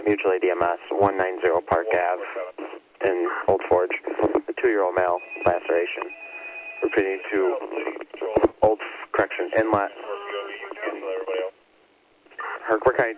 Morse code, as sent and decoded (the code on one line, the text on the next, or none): none